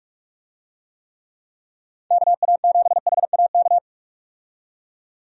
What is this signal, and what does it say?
Morse code, heard as a keyed tone, about 44 wpm, 700 Hz. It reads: KA7HAK
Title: USA Call Signs - 6 Characters - 45wpm